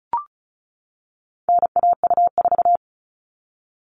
Morse code, heard as a keyed tone, about 34 wpm, 700 Hz. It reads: NAU4
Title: All Cut Numbers - 4-Digit 35wpm